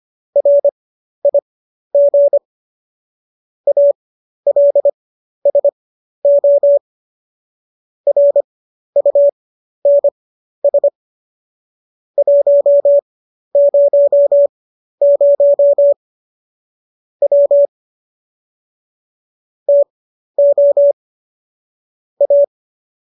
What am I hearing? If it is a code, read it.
Morse code, heard as a keyed tone, about 25 wpm, spaced as for 12 wpm, 570 Hz. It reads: RIG ALSO RUNS 100 W TO A